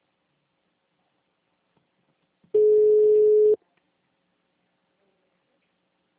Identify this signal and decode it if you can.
Ringback tone (Europe, 425 Hz).